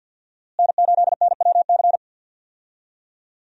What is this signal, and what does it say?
Morse code, heard as a keyed tone, about 46 wpm, 700 Hz. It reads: N8NWC
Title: USA Call Signs - 5 Characters - Repeated 3x - 50wpm